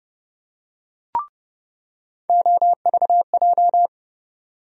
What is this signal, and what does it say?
Morse code, heard as a keyed tone, about 30 wpm, 700 Hz. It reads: OVJ
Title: USA Call Sign Postfixes - 3 Letters 30wpm